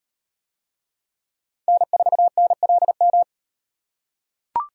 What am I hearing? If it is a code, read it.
Morse code, heard as a keyed tone, about 34 wpm, 700 Hz. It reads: N4NLM